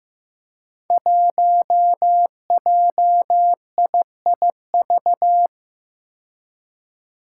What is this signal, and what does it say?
Morse code, heard as a keyed tone, about 15 wpm, 700 Hz. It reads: 1JIIV
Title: Five Challenging Letter-Number Combinations 15wpm